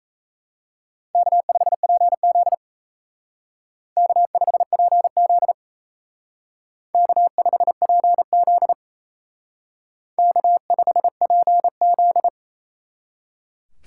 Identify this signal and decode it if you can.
Morse code, heard as a keyed tone, about 36 wpm, 700 Hz. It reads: K5PZ K5PZ K5PZ K5PZ